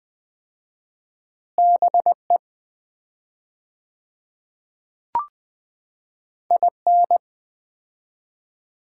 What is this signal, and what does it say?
Morse code, heard as a keyed tone, about 20 wpm, 700 Hz. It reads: BE IN
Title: 4x Word-Spacing - Sets of 2 Words - 20wpm